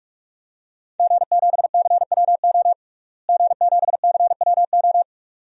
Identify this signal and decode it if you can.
Morse code, heard as a keyed tone, about 42 wpm, 700 Hz. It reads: G7CWO G7CWO
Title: DXCC-100 Call Signs - 5 Characters - Repeated 2x - 45wpm